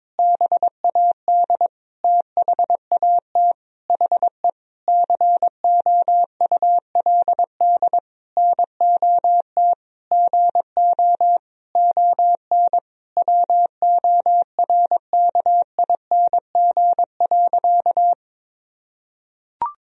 Morse code, heard as a keyed tone, about 22 wpm, 700 Hz. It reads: BAD THAT HE COULD NOT GO ON WORKING.
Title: Sentences from Top 1000 Words - Review Part 4 - 22wpm